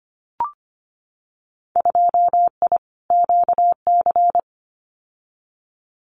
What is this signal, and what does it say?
Morse code, heard as a keyed tone, about 25 wpm, 700 Hz. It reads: 2I QC
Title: Winter Field Day Exchanges 25wpm